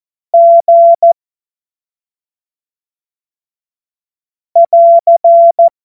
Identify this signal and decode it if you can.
Morse code, heard as a keyed tone, about 14 wpm, 680 Hz. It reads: G <AR>
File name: Morsekurs / Witze / Witz 13 Februar 2024 25 F70